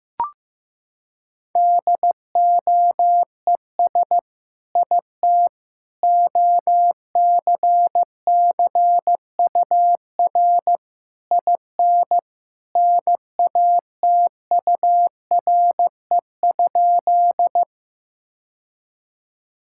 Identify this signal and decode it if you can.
Morse code, heard as a keyed tone, about 15 wpm, 700 Hz. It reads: DOES IT OCCUR IN NATURE?
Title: Sentences from Top 1000 Words - Review Part 6 - 15wpm